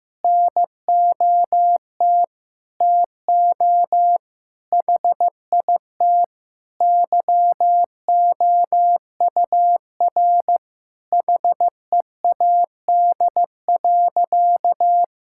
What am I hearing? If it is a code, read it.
Morse code, heard as a keyed tone, about 15 wpm, 700 Hz. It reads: NOT TO HIT YOUR HEAD.
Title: Sentences from Top 1600 Words 15wpm